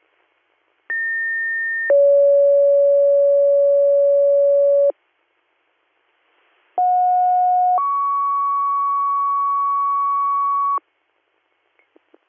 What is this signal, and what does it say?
Two-tone page, heard as a one-second tone, then a three-second tone, about 1800 then 560 Hz, about 730 then 1120 Hz. It is heard from 0.9 to 10.8 s.